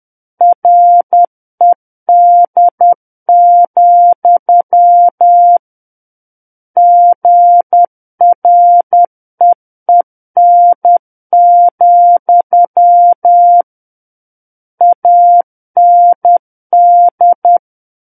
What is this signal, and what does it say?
Morse code, heard as a keyed tone, about 10 wpm, 700 Hz. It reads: RED, GREEN, AND
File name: Book One / 01-Chapter 1 - 10 WPM